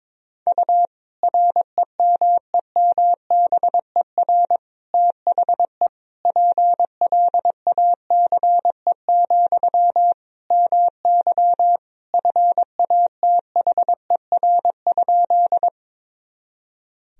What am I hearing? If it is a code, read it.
Morse code, heard as a keyed tone, about 22 wpm, 700 Hz. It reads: U REMEMBER THE PLACE, MY FATHER?
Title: Sentences from Top 400 Words and Abbreviations 22wpm